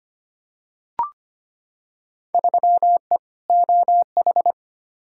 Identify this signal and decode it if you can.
Morse code, heard as a keyed tone, about 25 wpm, 700 Hz. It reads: 3E OH
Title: Field Day Exchanges - 25wpm